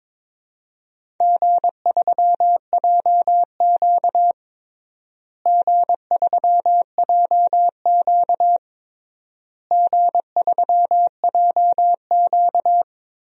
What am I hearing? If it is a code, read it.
Morse code, heard as a keyed tone, about 22 wpm, 700 Hz. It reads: G3JQ G3JQ G3JQ
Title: DXCC-100 Call Signs - 4 Characters - Repeated 3x - 22wpm